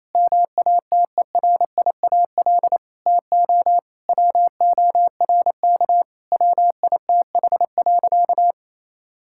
Morse code, heard as a keyed tone, about 28 wpm, 700 Hz. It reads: MATERIAL TO WORK WITH.
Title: Sentences from Top 1000 Words - Review Part 5 - 28wpm